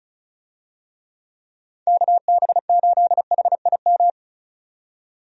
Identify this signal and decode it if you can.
Morse code, heard as a keyed tone, about 36 wpm, 700 Hz. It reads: KB8HIM
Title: USA Call Signs - 6 Characters - 35wpm